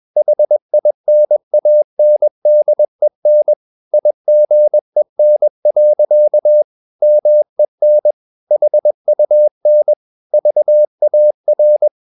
Morse code, heard as a keyed tone, about 21 wpm, 600 Hz. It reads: HINANDEN IGEN. MEN HUN VAR
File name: Ferie QRS 21wpm